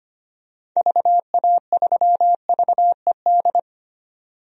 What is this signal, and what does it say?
Morse code, heard as a keyed tone, about 25 wpm, 700 Hz. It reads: VA3VED
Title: DXCC-100 Call Signs - 6 Characters - 25wpm